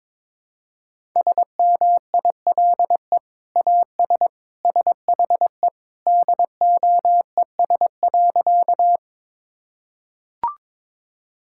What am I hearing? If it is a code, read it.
Morse code, heard as a keyed tone, about 22 wpm, 700 Hz. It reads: SMILE AS SHE DOES.
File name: practice / 22wpm / Sentences with 2-Word Phrases - 22wpm